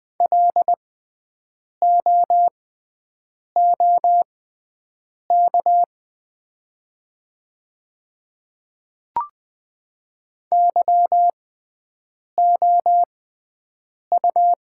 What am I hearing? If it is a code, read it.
Morse code, heard as a keyed tone, about 20 wpm, spaced as for 7 wpm, 700 Hz. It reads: LOOK YOU
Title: 6x Character Spacing - Top 100 Words - 20wpm